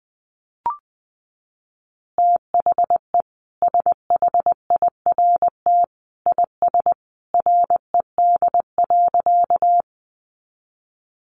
Morse code, heard as a keyed tone, about 19 wpm, 700 Hz. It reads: THE SHIRT IS RED.
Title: Additional Easy Phrases 20wpm